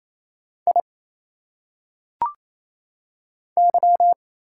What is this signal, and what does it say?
Morse code, heard as a keyed tone, about 27 wpm, 700 Hz. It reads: I Y
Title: Single Letters - Phonetic Pronunciation 28wpm